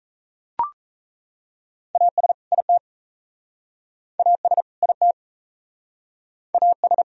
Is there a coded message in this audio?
Yes: Morse code from 2.0 to 7.0 s, heard as a keyed tone, about 38 wpm, 700 Hz. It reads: AS IT AS IT AS